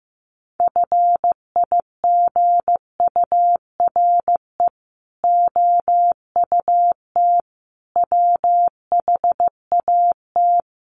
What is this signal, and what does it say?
Morse code, heard as a keyed tone, about 15 wpm, 700 Hz. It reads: FIGURE OUT WHAT
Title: Sets of 5 Words Encore2 15wpm